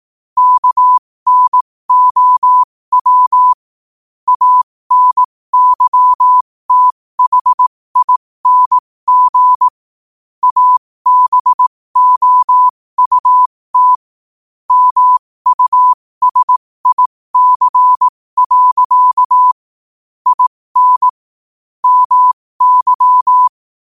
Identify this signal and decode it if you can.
Morse code, heard as a keyed tone, about 18 wpm, spaced as for 15 wpm, 1000 Hz. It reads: KNOW ANYTHING ABOUT MUSIC. IN MY
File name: qotdfiles / 15wpm